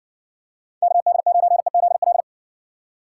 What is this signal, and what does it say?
Morse code, heard as a keyed tone, about 62 wpm, 700 Hz. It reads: KD9EZL